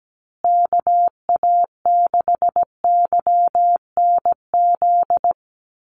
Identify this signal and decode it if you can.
Morse code, heard as a keyed tone, about 17 wpm, 700 Hz. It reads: KA6YNZ
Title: USA Call Signs Repeated 2x 17wpm